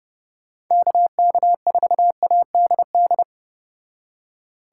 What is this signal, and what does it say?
Morse code, heard as a keyed tone, about 30 wpm, 700 Hz. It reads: KK4ADD